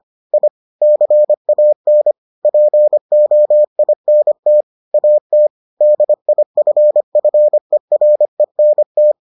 Morse code, heard as a keyed tone, about 25 wpm, 600 Hz. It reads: I CAN POINT AT DIFFERENT